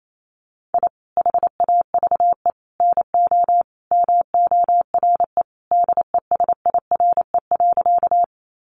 Morse code, heard as a keyed tone, about 28 wpm, 700 Hz. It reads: I HAVE NO MORE DESIRE.
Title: Sentences from Top 1100 Words 28wpm